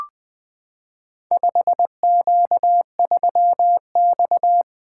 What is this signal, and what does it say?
Morse code, heard as a keyed tone, about 20 wpm, 700 Hz. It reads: 5Q3X
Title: Four Letter-Number Combinations - 20wpm